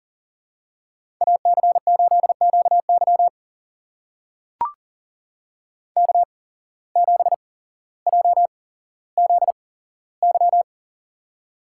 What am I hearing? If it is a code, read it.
Morse code, heard as a keyed tone, about 40 wpm, spaced as for 17 wpm, 700 Hz. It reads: AC8QY K7JZY
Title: USA Call Signs - 5 Characters - 8x Character Spacing - 40wpm